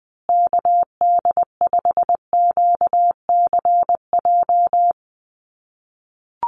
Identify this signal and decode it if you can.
Morse code, heard as a keyed tone, about 20 wpm, 700 Hz. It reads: KD5QCJ
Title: USA Call Signs Repeated 3x 20wpm